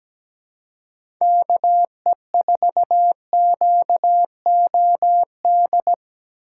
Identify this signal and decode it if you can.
Morse code, heard as a keyed tone, about 17 wpm, 700 Hz. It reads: KE4QOD
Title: USA Call Signs - 6 Characters - 3x Character Spacing - 17wpm